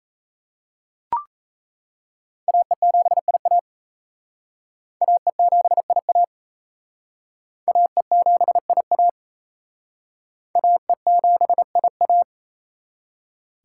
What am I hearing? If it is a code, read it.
Morse code, heard as a keyed tone, about 34 wpm, 700 Hz. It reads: AE7IA AE7IA AE7IA AE7IA
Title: Speed Racing - USA 2x2 Call Signs 28wpm